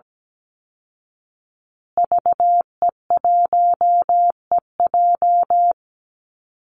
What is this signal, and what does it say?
Morse code, heard as a keyed tone, about 17 wpm, 700 Hz. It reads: VE1EJ
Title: Call Signs Repeated 3x - 17wpm